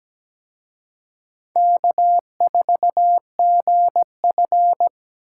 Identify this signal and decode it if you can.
Morse code, heard as a keyed tone, about 20 wpm, 700 Hz. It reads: K4GF
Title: Speed Racing - USA 1x2 Call Signs 17wpm